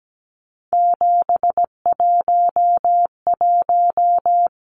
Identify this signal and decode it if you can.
Morse code, heard as a keyed tone, about 17 wpm, 700 Hz. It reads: 711
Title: Numbers - 3-Digit 17wpm